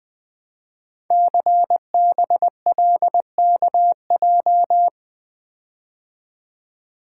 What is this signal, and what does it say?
Morse code, heard as a keyed tone, about 20 wpm, 700 Hz. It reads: CBLKJ
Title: Code Groups - 5 Letters - 20wpm